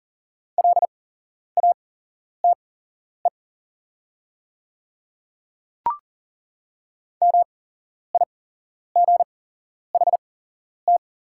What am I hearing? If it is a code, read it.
Morse code, heard as a keyed tone, about 40 wpm, spaced as for 11 wpm, 700 Hz. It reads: LATE MIGHT
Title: 8x Character Spacing - Top 300 Words - 40wpm